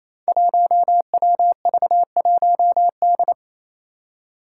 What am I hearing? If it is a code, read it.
Morse code, heard as a keyed tone, about 28 wpm, 700 Hz. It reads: 1WV1D
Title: Five Challenging Letter-Number Combinations 28wpm